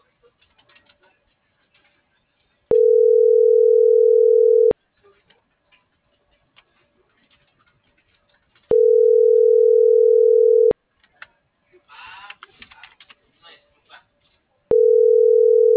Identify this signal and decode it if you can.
Ringback tone (North America), 440 + 480 Hz.